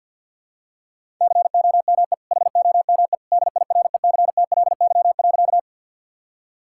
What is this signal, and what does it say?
Morse code, heard as a keyed tone, about 52 wpm, 700 Hz. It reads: COME HOME DIRECTLY.
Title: Sentences from Top 1000 Words - Review Part 5 - 50wpm